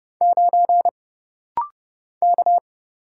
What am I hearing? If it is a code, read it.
Morse code, heard as a keyed tone, about 30 wpm, 700 Hz. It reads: 9 K
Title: Single Letter-Number - Rapid-Fire 30wpm